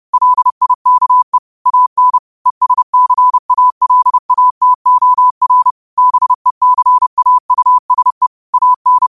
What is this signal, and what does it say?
Morse code, heard as a keyed tone, about 30 wpm, 1000 Hz. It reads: LIKE AN ESCALATOR BECAUSE AN
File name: qotdfiles / 30wpm